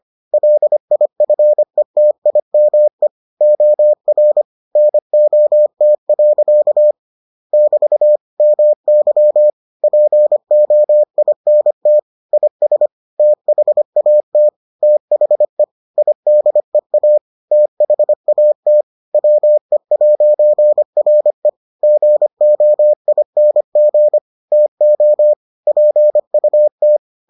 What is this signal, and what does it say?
Morse code, heard as a keyed tone, about 25 wpm, 600 Hz. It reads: LIFETIME OR NOT. <BT> MY POINT IS THAT THE IDEA THAT WE'RE GOING TO PUT